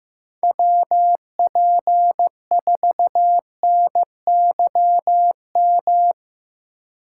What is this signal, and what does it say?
Morse code, heard as a keyed tone, about 17 wpm, 700 Hz. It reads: WP4NYM
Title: Speed Racing - USA Call Signs 15wpm